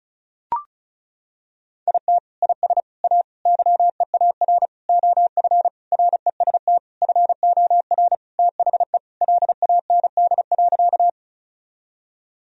Morse code, heard as a keyed tone, about 36 wpm, 700 Hz. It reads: IT IS A YEAR OF REST FOR THE LAND.
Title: Sentences from Top 1000 Words - Review Part 1 - 35wpm